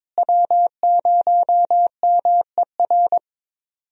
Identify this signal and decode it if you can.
Morse code, heard as a keyed tone, about 22 wpm, 700 Hz. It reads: W0MER
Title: USA Call Signs - 5 Characters - Repeated 3x - 22wpm